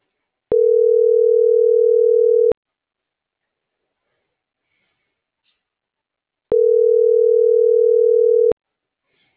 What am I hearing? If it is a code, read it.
Ringback tone (North America), 440 + 480 Hz.